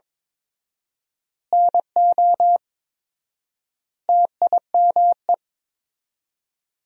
Morse code, heard as a keyed tone, about 22 wpm, 700 Hz. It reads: NO TIME
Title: 4x Word-Spacing - Sentences from Top 400 Words - 22wpm